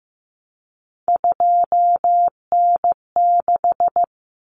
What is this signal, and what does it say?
Morse code, heard as a keyed tone, about 15 wpm, 700 Hz. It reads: 2N6
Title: Common Cut Numbers - 3-Digit 15wpm